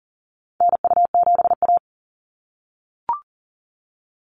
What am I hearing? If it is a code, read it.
Morse code, heard as a keyed tone, about 41 wpm, 700 Hz. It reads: NU7A
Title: NA SOTA Activator Callsigns 40wpm